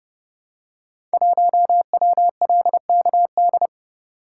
Morse code, heard as a keyed tone, about 30 wpm, 700 Hz. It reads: 1WLKD
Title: Code Groups - 5 Letter:Numbers - 30wpm